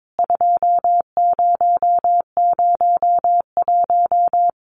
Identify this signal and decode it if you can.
Morse code, heard as a keyed tone, about 22 wpm, 700 Hz. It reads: 2001
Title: CWT Exchange 22wpm